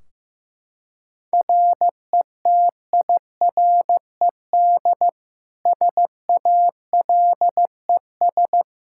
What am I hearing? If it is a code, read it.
Morse code, heard as a keyed tone, about 16 wpm, 700 Hz. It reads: RETIRED SALES
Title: Practice QSO - Exchange 3 - More Info - 15wpm